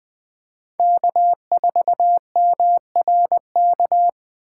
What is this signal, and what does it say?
Morse code, heard as a keyed tone, about 20 wpm, 700 Hz. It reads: K4MRK
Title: USA Call Signs - 6x Character Spacing - 20wpm